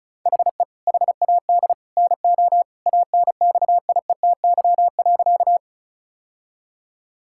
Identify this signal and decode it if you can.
Morse code, heard as a keyed tone, about 34 wpm, 700 Hz. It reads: HE HAD NO ANXIETY.
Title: Sentences from Top 1600 Words 35wpm